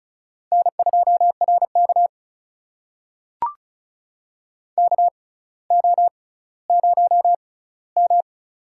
Morse code, heard as a keyed tone, about 35 wpm, spaced as for 18 wpm, 700 Hz. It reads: N2RK KO0M
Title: USA Call Signs - 4 Characters - 6x Character Spacing - 35wpm